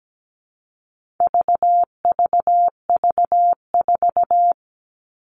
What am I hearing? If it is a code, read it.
Morse code, heard as a keyed tone, about 17 wpm, 700 Hz. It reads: VVV4
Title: All Cut Numbers - 4-Digit 17wpm